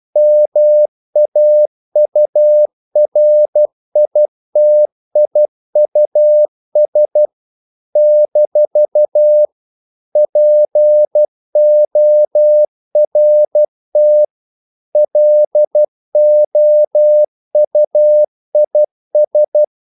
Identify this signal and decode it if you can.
Morse code, heard as a keyed tone, about 12 wpm, 600 Hz. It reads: MAURITIUS - PORT LOUIS
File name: CW / Practice / NationalCapitals12